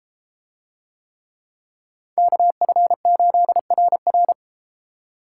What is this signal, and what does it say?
Morse code, heard as a keyed tone, about 31 wpm, 700 Hz. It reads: KF8RR